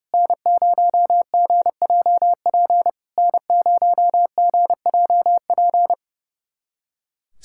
Morse code, heard as a keyed tone, about 30 wpm, 690 Hz. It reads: N0GJP N0GJP
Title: SOTA QSO Practice 30wpm